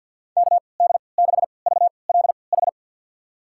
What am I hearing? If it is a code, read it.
Morse code, heard as a keyed tone, about 52 wpm, spaced as for 29 wpm, 690 Hz. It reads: KD6VLH
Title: USA Call Signs - 6 Characters - 3x Character Spacing - 50wpm